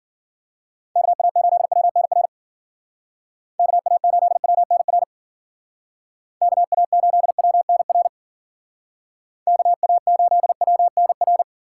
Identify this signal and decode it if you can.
Morse code, heard as a keyed tone, about 50 wpm, 700 Hz. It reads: KA8WNR KA8WNR KA8WNR KA8WNR
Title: Speed Racing - USA 2x3 Call Signs 40wpm